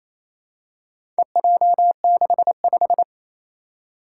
Morse code, heard as a keyed tone, about 28 wpm, 700 Hz. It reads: EJ65
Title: Four Challenging Letter-Number Combinations 28wpm